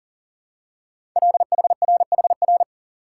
Morse code, heard as a keyed tone, about 39 wpm, 700 Hz. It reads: LHRHR
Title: Code Groups - 5 Letter:Numbers - 40wpm